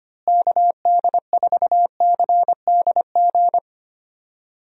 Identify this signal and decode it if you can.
Morse code, heard as a keyed tone, about 25 wpm, 700 Hz. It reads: KD4CDG
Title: USA Call Signs - 6 Characters - Repeated 2x - 25wpm